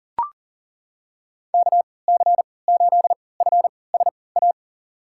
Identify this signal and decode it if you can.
Morse code, heard as a keyed tone, about 40 wpm, spaced as for 23 wpm, 700 Hz. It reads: KC8FSA
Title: USA Call Signs - 3x Character Spacing - 40wpm